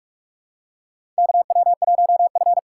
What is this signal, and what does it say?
Morse code, heard as a keyed tone, about 44 wpm, 700 Hz. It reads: KW1F
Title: USA Call Signs - 4 Characters - 6x Character Spacing - 45wpm